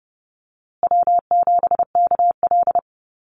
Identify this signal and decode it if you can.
Morse code, heard as a keyed tone, about 30 wpm, 700 Hz. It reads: W7KL
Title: USA Call Signs Repeated 2x 30wpm